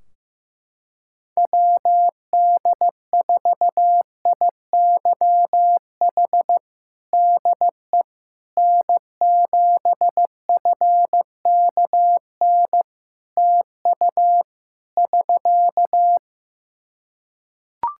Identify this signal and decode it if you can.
Morse code, heard as a keyed tone, about 16 wpm, 700 Hz. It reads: WD4IYH DE N7FKN TU <SK>
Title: Practice QSO - Exchange 4 - Ending - 15wpm